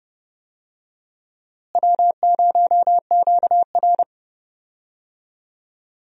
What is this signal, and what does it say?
Morse code, heard as a keyed tone, about 30 wpm, 700 Hz. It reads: W0QR
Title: USA Call Signs - 4 Characters - Repeated 2x - 30wpm